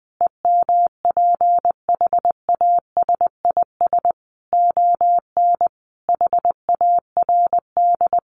Morse code, heard as a keyed tone, about 20 wpm, 700 Hz. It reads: EMPHASIS ON HARD
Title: Sentences from Top 1400 Words 20wpm